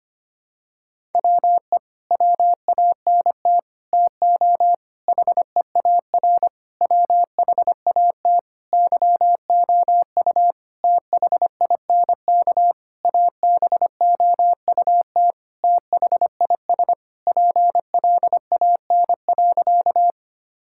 Morse code, heard as a keyed tone, about 25 wpm, 700 Hz. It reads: WE WANT TO HEAR WHAT YOU THINK ABOUT THIS PLAN.
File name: practice / 25wpm / Sentences with 9-Word Phrases - 25wpm